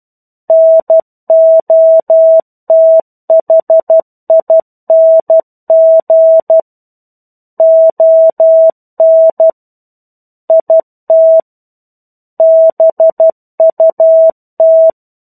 Morse code, heard as a keyed tone, about 12 wpm, 650 Hz. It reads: NOTHING ON IT BUT
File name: 01-Chapter-01 12 WPM